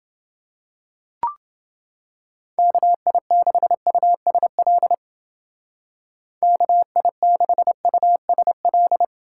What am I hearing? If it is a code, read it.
Morse code, heard as a keyed tone, about 31 wpm, 700 Hz. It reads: KI6USL KI6USL